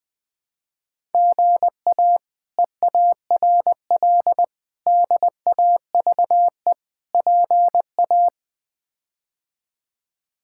Morse code, heard as a keyed tone, about 21 wpm, 700 Hz. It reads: GA EARL DAVE PA
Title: K1USN SST QSO Practice - 20wpm